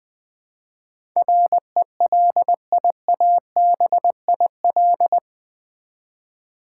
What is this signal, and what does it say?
Morse code, heard as a keyed tone, about 20 wpm, 700 Hz. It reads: RELIABIL
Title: Build A Word - 4 Letters At A Time - 11-Letter Words - 20wpm